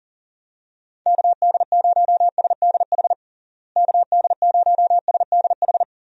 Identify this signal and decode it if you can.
Morse code, heard as a keyed tone, about 40 wpm, 700 Hz. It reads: KD0SDH KD0SDH
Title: USA Call Signs - 6 Characters - Repeated 3x - 40wpm